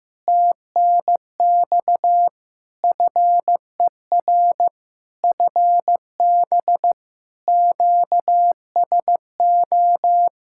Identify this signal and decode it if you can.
Morse code, heard as a keyed tone, about 16 wpm, 700 Hz. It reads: TNX FER FB QSO
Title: K1USN SST QSO Practice 15wpm